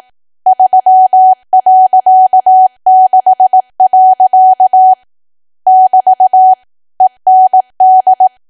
Morse code, heard as a keyed tone, about 18 wpm, 750 Hz. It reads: 3.6. <BT> END
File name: cwa / W1AW-Level2 / 18WPM